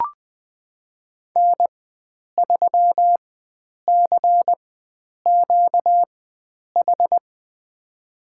Morse code, heard as a keyed tone, about 20 wpm, spaced as for 9 wpm, 700 Hz. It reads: N3CQH
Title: USA Call Signs - 5 Characters - 4x Character Spacing - 20wpm